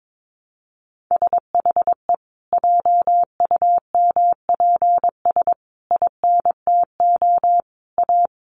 Morse code, heard as a keyed tone, about 22 wpm, 700 Hz. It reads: SHE JUMPS INTO A
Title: Additional Easy Phrases 22wpm